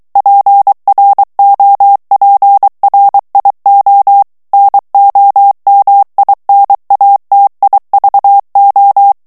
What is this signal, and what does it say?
Morse code, heard as a keyed tone, about 23 wpm, 800 Hz. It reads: PROPRIO NOMINATIVO
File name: ITALIEN texte clair 2 vitesse 22 mots minute